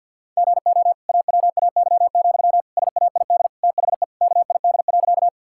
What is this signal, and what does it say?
Morse code, heard as a keyed tone, about 48 wpm, 700 Hz. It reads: GO AWAY, SAID THE KID.
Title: Sentences from Top 1600 Words 50wpm